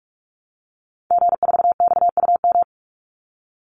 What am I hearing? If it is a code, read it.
Morse code, heard as a keyed tone, about 44 wpm, 700 Hz. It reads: G4XUM
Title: Call Signs - 45wpm